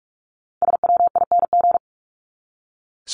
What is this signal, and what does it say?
Morse code, heard as a keyed tone, about 44 wpm, 700 Hz. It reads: SWING
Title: Top 1751-2000 Words 45wpm